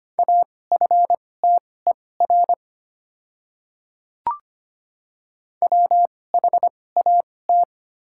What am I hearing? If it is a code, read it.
Morse code, heard as a keyed tone, about 25 wpm, spaced as for 18 wpm, 700 Hz. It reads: AFTER WHAT
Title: 2x Character Spacing - Top 300 Words - 25wpm